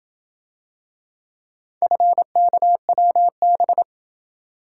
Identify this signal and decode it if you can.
Morse code, heard as a keyed tone, about 28 wpm, 700 Hz. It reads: FKWB